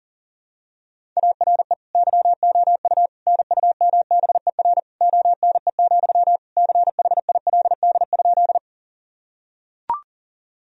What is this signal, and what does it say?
Morse code, heard as a keyed tone, about 40 wpm, 700 Hz. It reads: ARE YOU NUMBER ONE, CHILD?
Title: Sentences from Top 1000 Words - Review Part 1 - 40wpm